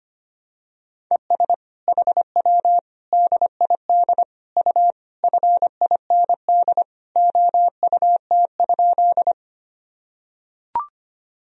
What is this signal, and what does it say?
Morse code, heard as a keyed tone, about 25 wpm, 700 Hz. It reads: ES HW DID U FIND OUT?
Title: Sentences from Top 100 Words and Abbreviations 25wpm